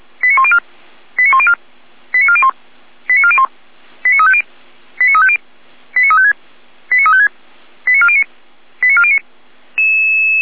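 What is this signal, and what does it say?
Five-tone selective call (ZVEI), heard as quick runs of five tones, tones of about 70 ms. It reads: 89104 89104 89401 89401 89350 89350 89356 89356 89409 89409
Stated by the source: ZVEI five-tone call received on SDR.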